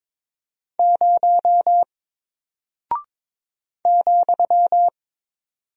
Morse code, heard as a keyed tone, about 22 wpm, 700 Hz. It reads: 0 ,
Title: Single Character (Letter:Number:Punctuation:Prosign) 22wpm